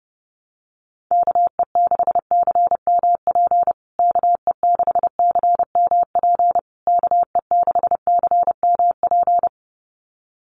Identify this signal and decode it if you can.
Morse code, heard as a keyed tone, about 30 wpm, 700 Hz. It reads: KE6CMP KE6CMP KE6CMP
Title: USA Call Signs Repeated 3x 30wpm